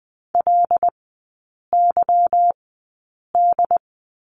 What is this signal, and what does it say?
Morse code, heard as a keyed tone, about 20 wpm, spaced as for 8 wpm, 700 Hz. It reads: LYD
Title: 178 Lesson 29e - Slash - Three Characters 20wpm